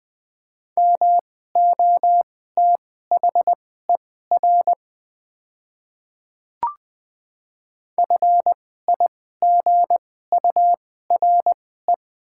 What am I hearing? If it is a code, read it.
Morse code, heard as a keyed tone, about 20 wpm, spaced as for 14 wpm, 690 Hz. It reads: MOTHER FIGURE
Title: 2x Character Spacing - Top 500 Words - 20wpm